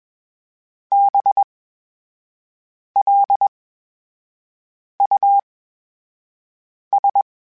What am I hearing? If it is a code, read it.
Morse code, heard as a keyed tone, about 21 wpm, spaced as for 5 wpm, 800 Hz. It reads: BLUS
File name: Special INGs-mp3s / Special INGs-mp3s / VerySlowToFaster Ing @20